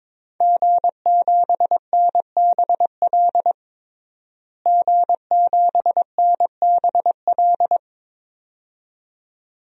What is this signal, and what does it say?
Morse code, heard as a keyed tone, about 22 wpm, 700 Hz. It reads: G7NBL G7NBL
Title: DXCC-100 Call Signs - 5 Characters - Repeated 2x - 22wpm